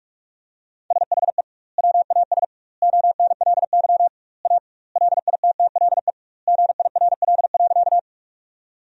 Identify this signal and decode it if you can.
Morse code, heard as a keyed tone, about 46 wpm, 700 Hz. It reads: SHE WAS ONLY A LITTLE GIRL.
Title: 2x Word-Spacing - Sentences from Top 400 Words - 45wpm